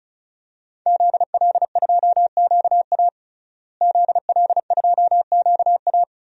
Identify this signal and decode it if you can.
Morse code, heard as a keyed tone, about 36 wpm, 700 Hz. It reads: ZL2QA ZL2QA
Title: DXCC-100 Call Signs - 5 Characters - Repeated 3x - 35wpm